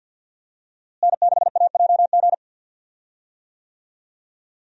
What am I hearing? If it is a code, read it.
Morse code, heard as a keyed tone, about 51 wpm, 700 Hz. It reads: N6AJG
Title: USA Call Signs - 5 Characters - Repeated 2x - 50wpm